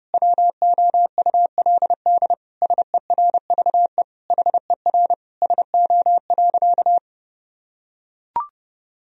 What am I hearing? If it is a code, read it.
Morse code, heard as a keyed tone, about 30 wpm, 700 Hz. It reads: WOULD SERVE HER SO.
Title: Sentences from Top 1000 Words - Review Part 2 - 30wpm